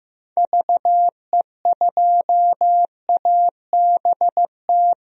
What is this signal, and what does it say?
Morse code, heard as a keyed tone, about 15 wpm, 700 Hz. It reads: VE2ABT
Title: DXCC-100 Call Signs - 6 Characters - Repeated 3x - 15wpm